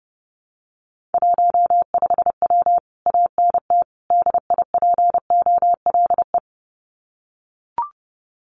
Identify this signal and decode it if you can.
Morse code, heard as a keyed tone, about 29 wpm, 700 Hz. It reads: 15W ANT DIPOLE
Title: Common QSO Phrases Encore 30wpm